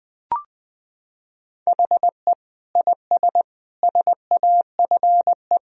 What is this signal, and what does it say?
Morse code, heard as a keyed tone, about 20 wpm, 700 Hz. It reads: HE IS SAFE
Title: Sentences from Top 1000 Words - Review Part 4 - 20wpm